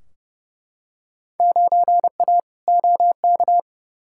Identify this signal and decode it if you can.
Morse code, heard as a keyed tone, about 30 wpm, 700 Hz. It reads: 9A OK
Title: Field Day Exchanges Encore - 30wpm